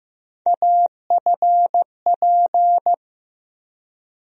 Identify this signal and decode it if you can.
Morse code, heard as a keyed tone, about 15 wpm, 700 Hz. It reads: AFP
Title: USA Call Sign Postfixes - 3 Letters 15wpm